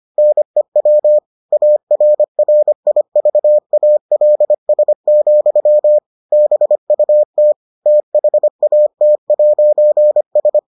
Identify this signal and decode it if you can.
Morse code, heard as a keyed tone, about 25 wpm, 600 Hz. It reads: NEW ARRIVALS, BUT THAT'S